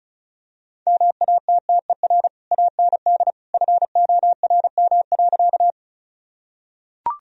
Morse code, heard as a keyed tone, about 36 wpm, 700 Hz. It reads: MATTER AND FORM.
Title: Sentences from Top 1000 Words - Review Part 1 - 35wpm